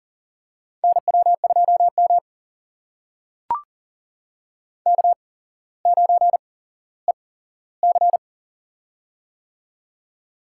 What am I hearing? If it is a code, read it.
Morse code, heard as a keyed tone, about 42 wpm, spaced as for 17 wpm, 700 Hz. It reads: NW2M K9EC
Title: USA Call Signs - 4 Characters - 8x Character Spacing - 40wpm